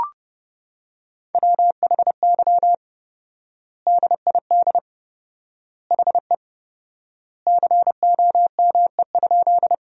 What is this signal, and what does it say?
Morse code, heard as a keyed tone, about 30 wpm, 700 Hz. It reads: WHY DID HE COME?
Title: 4x Word-Spacing - Sentences from Top 200 Words - 30wpm